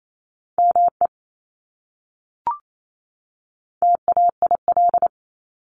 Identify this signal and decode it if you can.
Morse code, heard as a keyed tone, about 28 wpm, 700 Hz. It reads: ME TAIL
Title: Top 500 Words 28wpm